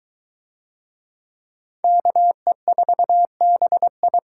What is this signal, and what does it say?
Morse code, heard as a keyed tone, about 21 wpm, 700 Hz. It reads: KE4BI